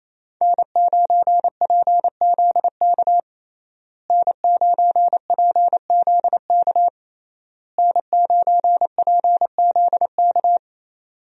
Morse code, heard as a keyed tone, about 27 wpm, 700 Hz. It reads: N9PZK N9PZK N9PZK
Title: USA Call Signs - 5 Characters - Repeated 3x - 28wpm